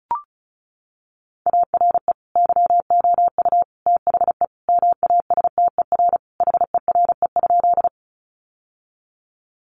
Morse code, heard as a keyed tone, about 34 wpm, 700 Hz. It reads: ARE YOU THE MASTER HERE?
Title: Sentences from Top 1300 Words 35wpm